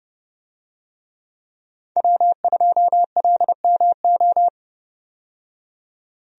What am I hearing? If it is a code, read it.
Morse code, heard as a keyed tone, about 30 wpm, 700 Hz. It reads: W2LMO
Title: USA Call Signs - 5 Characters - 30wpm